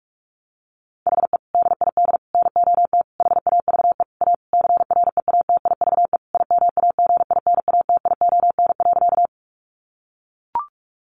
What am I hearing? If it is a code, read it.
Morse code, heard as a keyed tone, about 44 wpm, 700 Hz. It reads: HE DID NOT HAVE A CREATIVE IMAGINATION.
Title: Sentences from Top 1300 Words 45wpm